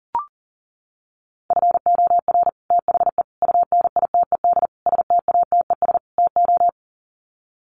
Morse code, heard as a keyed tone, about 38 wpm, 700 Hz. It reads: FOR THE UNITED STATES TO
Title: Sets of 5 Words Encore 40wpm